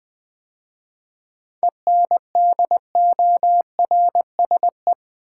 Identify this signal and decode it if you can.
Morse code, heard as a keyed tone, about 20 wpm, 700 Hz. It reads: ENDORSE
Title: 7 Letter Words - First 3 Letters Given - 20wpm